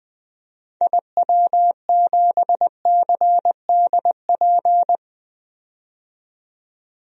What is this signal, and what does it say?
Morse code, heard as a keyed tone, about 20 wpm, 700 Hz. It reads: IW7CDP
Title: DXCC-100 Call Signs - 6 Characters - 20wpm